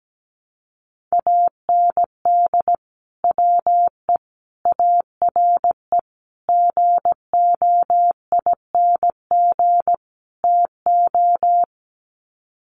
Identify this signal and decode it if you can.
Morse code, heard as a keyed tone, about 17 wpm, 700 Hz. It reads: AND WE ARE GOING TO
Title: Sets of 5 Words Encore2 17wpm